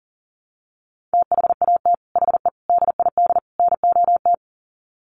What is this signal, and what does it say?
Morse code, heard as a keyed tone, about 40 wpm, 700 Hz. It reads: THAT HE DID NOT
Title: Sets of 4 Words 40wpm